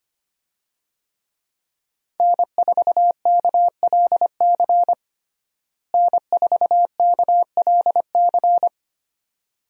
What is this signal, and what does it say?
Morse code, heard as a keyed tone, about 25 wpm, 700 Hz. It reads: N4KLC N4KLC
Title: USA Call Signs - 5 Characters - Repeated 2x - 25wpm